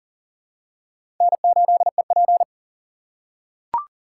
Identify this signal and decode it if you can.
Morse code, heard as a keyed tone, about 40 wpm, 700 Hz. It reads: N8EP
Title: USA Call Signs - 2x Character Spacing - 40wpm